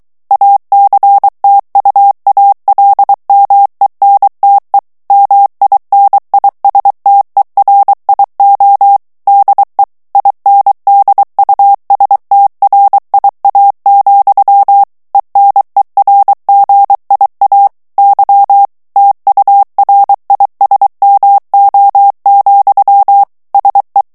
Morse code, heard as a keyed tone, about 23 wpm, 800 Hz. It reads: ACTUALMENTE MINISTERIO DE INDUSTRIA, ENERGIA Y TURISMO, SE
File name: morse mp3 / ESPAGNOL TEXTE EN CLAIR 2 VITESSE 22 MOTS MINUTES